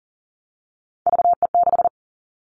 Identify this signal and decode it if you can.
Morse code, heard as a keyed tone, about 40 wpm, 700 Hz. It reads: VE6
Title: All Cut Numbers - 3-Digit 40wpm